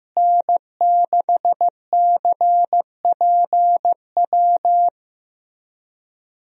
Morse code, heard as a keyed tone, about 15 wpm, 700 Hz. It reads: N6CPW
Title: USA Call Signs - 5 Characters - Repeated 2x - 15wpm